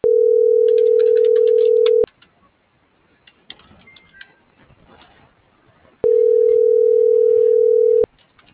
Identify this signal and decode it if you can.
Ringback tone (North America), 440 + 480 Hz.